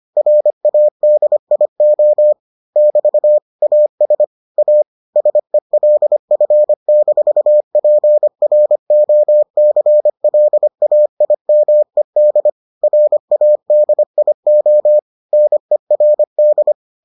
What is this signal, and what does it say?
Morse code, heard as a keyed tone, about 25 wpm, 600 Hz. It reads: RADIO <BT> AS A SELF-PROCLAIMED RADIO NERD